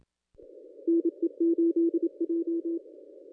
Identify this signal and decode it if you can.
Morse code, heard as a keyed tone, about 26 wpm, 400 Hz. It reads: NE8J
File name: contests / ss-cw-k5tr-op-n5rz -1100